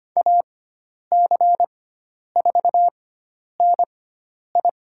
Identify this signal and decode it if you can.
Morse code, heard as a keyed tone, about 25 wpm, spaced as for 10 wpm, 700 Hz. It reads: AC4NI
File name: Lesson 16g - Letter c - Call Signs 25wpm